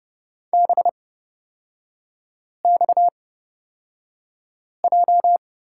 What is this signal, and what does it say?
Morse code, heard as a keyed tone, about 31 wpm, spaced as for 5 wpm, 690 Hz. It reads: BXJ
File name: Lesson 37e - Letter x - Three Characters 30wpm